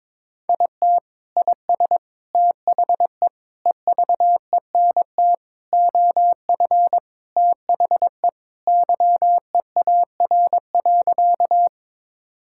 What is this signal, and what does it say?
Morse code, heard as a keyed tone, about 22 wpm, 700 Hz. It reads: IT IS THE EVENT OF THE YEAR.
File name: practice / 22wpm / Sentences with 3-Word Phrases - 22wpm